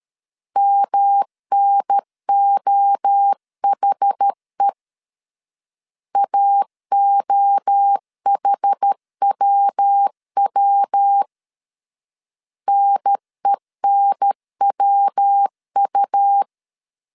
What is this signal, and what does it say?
Morse code, heard as a keyed tone, about 13 wpm, 800 Hz. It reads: MNOHE AOHWW NENWU
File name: CW / DET / 1 3 1500C S B 12wpm 2828